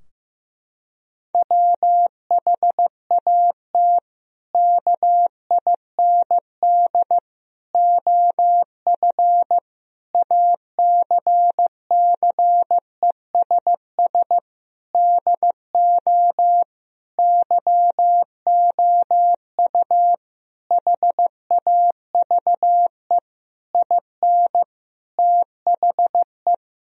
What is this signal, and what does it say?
Morse code, heard as a keyed tone, about 15 wpm, 700 Hz. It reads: WHAT KIND OF ACCESS DO YOU HAVE IN THE
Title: Sentences from Top 1000 Words - Review Part 4 - 15wpm